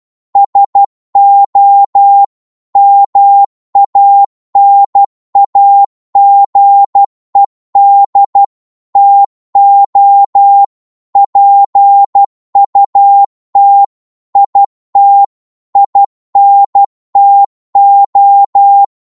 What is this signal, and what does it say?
Morse code, heard as a keyed tone, about 12 wpm, 800 Hz. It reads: SO MANAGED TO PUT IT INTO
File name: cwmp3 / psg-1-12wpm